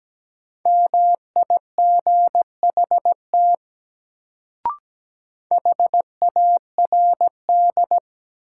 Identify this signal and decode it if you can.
Morse code, heard as a keyed tone, about 17 wpm, 700 Hz. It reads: MIGHT HARD
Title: Sentences from Top 300 Words - New Word Drill - 17wpm